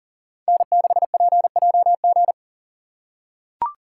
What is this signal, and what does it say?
Morse code, heard as a keyed tone, about 40 wpm, 690 Hz. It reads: N6PJG
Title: USA Call Signs - 5 Characters - 6x Character Spacing - 40wpm